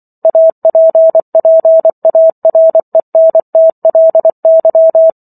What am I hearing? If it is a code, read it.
Morse code, heard as a keyed tone, about 24 wpm, 650 Hz. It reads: APPARENTLY
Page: War of the Worlds - 15-Chapter 15 - 24 WPM